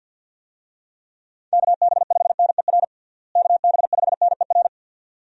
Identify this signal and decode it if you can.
Morse code, heard as a keyed tone, about 53 wpm, 700 Hz. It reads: KB5NER KB5NER
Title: USA Call Signs - 6 Characters - Repeated 2x - 50wpm